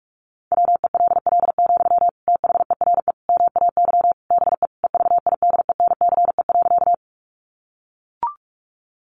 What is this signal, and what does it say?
Morse code, heard as a keyed tone, about 44 wpm, 700 Hz. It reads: WELL, THERE MAY BE EVIDENCE.
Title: Sentences from Top 1300 Words 45wpm